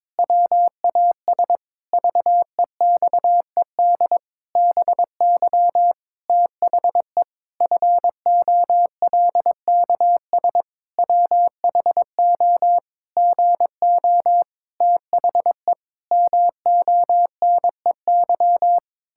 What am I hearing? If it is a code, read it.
Morse code, heard as a keyed tone, about 22 wpm, 700 Hz. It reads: WAS VEXED BY THE FOLKS WHO GO THE MONEY
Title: Pangram Challenge v2 - 22wpm